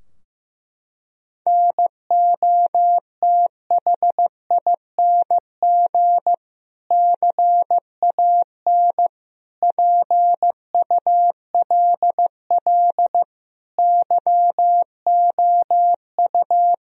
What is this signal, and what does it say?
Morse code, heard as a keyed tone, about 16 wpm, 700 Hz. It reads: NOTHING CAN PULL YOU
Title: Sentences from Top 1000 Words - Review Part 2 - 15wpm